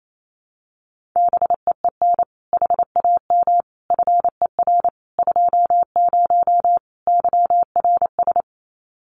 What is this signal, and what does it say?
Morse code, heard as a keyed tone, about 28 wpm, 700 Hz. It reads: BEEN HAM FER 20 YRS
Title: Common QSO Phrases 28wpm